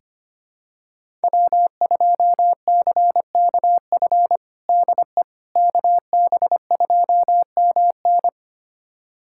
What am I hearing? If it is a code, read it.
Morse code, heard as a keyed tone, about 25 wpm, 700 Hz. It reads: W2CKF DE KB2MN
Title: USA Callsign DE Callsign - 5 Characters - 25wpm